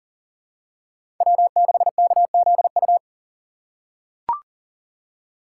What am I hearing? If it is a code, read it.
Morse code, heard as a keyed tone, about 40 wpm, 700 Hz. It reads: W6KZU
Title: USA Call Signs - 5 Characters - 2x Character Spacing - 40wpm